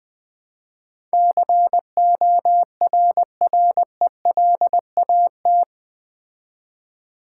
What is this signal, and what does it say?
Morse code, heard as a keyed tone, about 20 wpm, 700 Hz. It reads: CORRELAT
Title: Build A Word - 4 Letters At A Time - 11-Letter Words - 20wpm